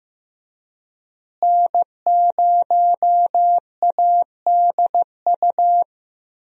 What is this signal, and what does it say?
Morse code, heard as a keyed tone, about 15 wpm, 700 Hz. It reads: N0ADU
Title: USA Call Signs - 5 Characters - Repeated 2x - 15wpm